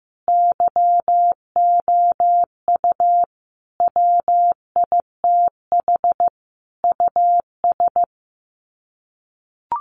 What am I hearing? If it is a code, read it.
Morse code, heard as a keyed tone, about 15 wpm, 700 Hz. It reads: YOU WITH US
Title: Sets of 5 Words 15wpm